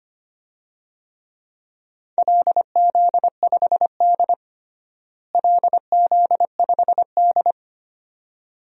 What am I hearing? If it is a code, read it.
Morse code, heard as a keyed tone, about 25 wpm, 700 Hz. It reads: LZ5D LZ5D
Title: DXCC-100 Call Signs - 4 Characters - Repeated 2x - 25wpm